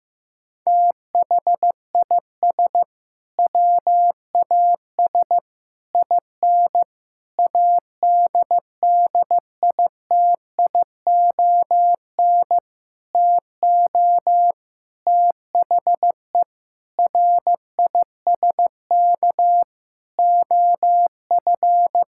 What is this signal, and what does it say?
Morse code, heard as a keyed tone, about 16 wpm, 700 Hz. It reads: THIS WAS IN ADDITION TO THE RISK OF
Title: Sentences from Top 1000 Words - Review Part 6 - 15wpm